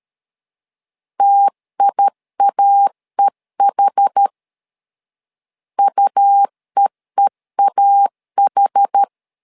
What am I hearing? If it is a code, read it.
Morse code, heard as a keyed tone, about 13 wpm, 800 Hz. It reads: TIAEH UEEAH
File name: LEZIONI / 2 UHDN / 2 UH 12wpm 1439